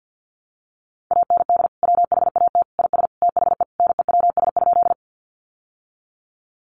Morse code, heard as a keyed tone, about 50 wpm, 700 Hz. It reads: AND WHAT IS THE NEWS?
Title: Sentences from Top 800 Words 50wpm